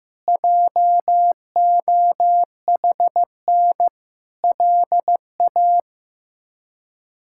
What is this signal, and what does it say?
Morse code, heard as a keyed tone, about 15 wpm, 700 Hz. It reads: JOHN LA
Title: K1USN SST QSO Practice - 15wpm